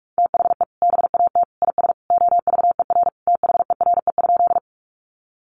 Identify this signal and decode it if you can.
Morse code, heard as a keyed tone, about 44 wpm, 700 Hz. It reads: THE BAT IS OVER THERE?
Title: Sentences from Top 1500 Words 45wpm